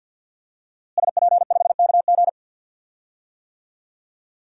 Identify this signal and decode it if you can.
Morse code, heard as a keyed tone, about 47 wpm, 700 Hz. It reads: SP5KG
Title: DXCC-100 Call Signs - 5 Characters - Repeated 2x - 50wpm